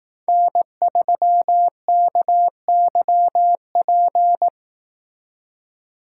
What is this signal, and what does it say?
Morse code, heard as a keyed tone, about 19 wpm, 700 Hz. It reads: N3KYP
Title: Speed Racing - USA 1x3 Call Signs 15wpm